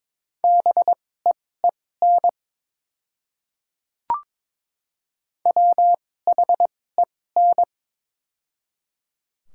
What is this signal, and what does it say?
Morse code, heard as a keyed tone, about 22 wpm, spaced as for 16 wpm, 690 Hz. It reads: BEEN WHEN
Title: 2x Character Spacing - Top 100 Words - 22wpm